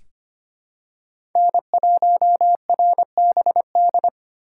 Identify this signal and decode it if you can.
Morse code, heard as a keyed tone, about 25 wpm, 700 Hz. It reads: N1RBD
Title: K1USN SST QSO Practice - 25wpm